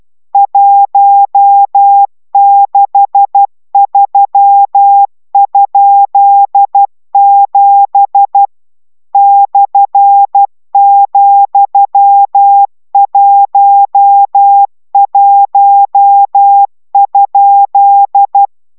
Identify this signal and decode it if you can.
Morse code, heard as a keyed tone, about 12 wpm, 800 Hz. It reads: 163?7 /,11?